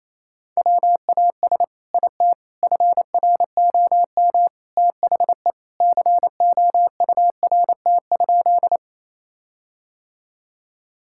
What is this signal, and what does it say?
Morse code, heard as a keyed tone, about 28 wpm, 700 Hz. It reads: WAS IT FROM THE COURT?
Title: Sentences from Top 1600 Words 28wpm